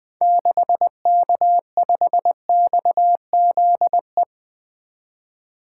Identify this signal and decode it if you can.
Morse code, heard as a keyed tone, about 20 wpm, 700 Hz. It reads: 6K5XZE
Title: 2023 SOTA Activator Call Signs - 20wpm